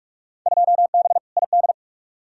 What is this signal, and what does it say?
Morse code, heard as a keyed tone, about 43 wpm, 700 Hz. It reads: 2B ID
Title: Field Day Exchanges Encore - 45wpm